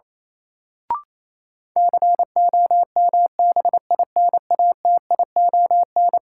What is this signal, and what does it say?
Morse code, heard as a keyed tone, about 27 wpm, 700 Hz. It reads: COMBINATION
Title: Sentences from Top 1300 Words - New Word Drill Alt - 28wpm